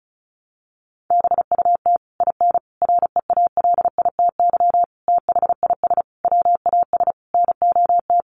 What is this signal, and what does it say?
Morse code, heard as a keyed tone, about 34 wpm, 700 Hz. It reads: BUT IN REALITY THIS WAS NOT
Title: Sentences from Top 1200 Words 35wpm